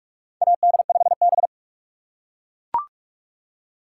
Morse code, heard as a keyed tone, about 43 wpm, 700 Hz. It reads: AD5B
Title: USA Call Signs - 4 Characters - 6x Character Spacing - 45wpm